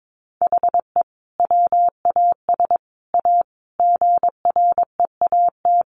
Morse code, heard as a keyed tone, about 22 wpm, 700 Hz. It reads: HE WAS A GREAT
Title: Sentences from Top 1200 Words 22wpm